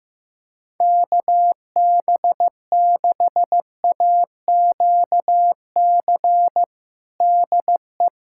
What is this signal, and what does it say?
Morse code, heard as a keyed tone, about 15 wpm, 700 Hz. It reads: KB6AQC DE
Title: K1USN SST QSO Practice 15wpm